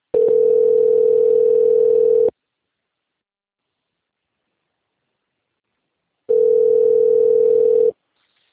Ringback tone (North America), 440 + 480 Hz.
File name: rg-1600-unknown- - - .34810